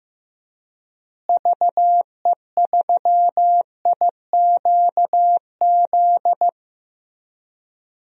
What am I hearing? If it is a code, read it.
Morse code, heard as a keyed tone, about 15 wpm, 700 Hz. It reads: VE3IQZ
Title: DXCC-100 Call Signs - 6 Characters - Repeated 3x - 15wpm